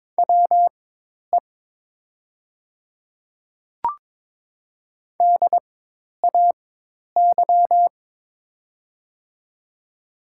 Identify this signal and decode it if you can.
Morse code, heard as a keyed tone, about 23 wpm, spaced as for 10 wpm, 700 Hz. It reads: WE DAY
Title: 4x Character Spacing - Top 100 Words - 22wpm